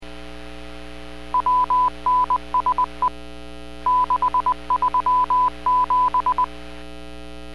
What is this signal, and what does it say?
Morse code, heard as a keyed tone, about 20 wpm, spaced as for 17 wpm, 1000 Hz. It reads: WNSE637